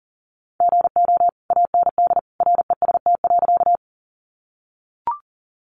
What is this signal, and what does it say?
Morse code, heard as a keyed tone, about 40 wpm, 700 Hz. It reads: GO AND REST.
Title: Sentences from Top 500 Words 40wpm